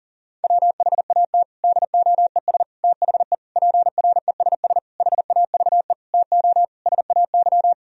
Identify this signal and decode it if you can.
Morse code, heard as a keyed tone, about 40 wpm, 700 Hz. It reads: WHAT DOES THE PRESS HAVE TO SAY
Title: Sentences from Top 1000 Words - Review Part 3 - 40wpm